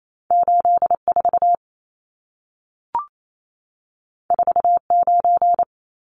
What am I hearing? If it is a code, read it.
Morse code, heard as a keyed tone, about 28 wpm, 700 Hz. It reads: 84 49
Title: Numbers - 2-Digit 28wpm